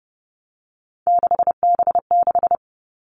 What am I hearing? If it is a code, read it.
Morse code, heard as a keyed tone, about 30 wpm, 700 Hz. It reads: 6B6
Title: All Cut Numbers - 3-Digit 30wpm